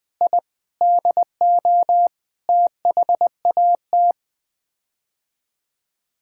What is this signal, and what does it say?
Morse code, heard as a keyed tone, about 20 wpm, 700 Hz. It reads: I DO THAT
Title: Sets of 8 Words - 20wpm